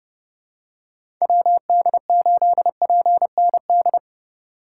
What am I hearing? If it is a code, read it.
Morse code, heard as a keyed tone, about 30 wpm, 700 Hz. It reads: WD8PND
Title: USA Call Signs - 4x Character Spacing - 30wpm